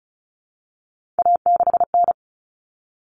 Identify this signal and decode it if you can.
Morse code, heard as a keyed tone, about 34 wpm, 700 Hz. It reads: A6N
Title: All Cut Numbers - 3-Digit 35wpm